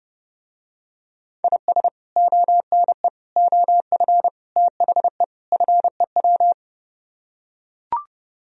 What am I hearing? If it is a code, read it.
Morse code, heard as a keyed tone, about 29 wpm, 700 Hz. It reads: IS ONE OF THE FEW